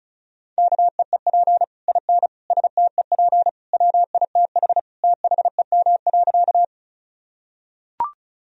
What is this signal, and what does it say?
Morse code, heard as a keyed tone, about 36 wpm, 700 Hz. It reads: KEEP IN STEP WITH THEM.
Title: Sentences from Top 1000 Words - Review Part 2 - 35wpm